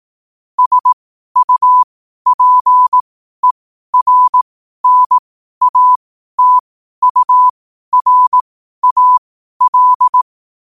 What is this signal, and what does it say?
Morse code, heard as a keyed tone, about 18 wpm, spaced as for 12 wpm, 1000 Hz. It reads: SUPERNATURAL
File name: qotdfiles / 12wpm